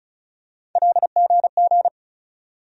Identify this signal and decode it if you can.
Morse code, heard as a keyed tone, about 36 wpm, 700 Hz. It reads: LGG
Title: USA Call Sign Postfixes - 3 Letters 35wpm